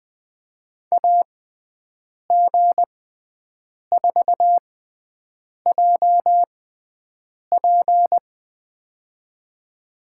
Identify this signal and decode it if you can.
Morse code, heard as a keyed tone, about 20 wpm, spaced as for 7 wpm, 700 Hz. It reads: AG4JP
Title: USA Call Signs - 5 Characters - 6x Character Spacing - 20wpm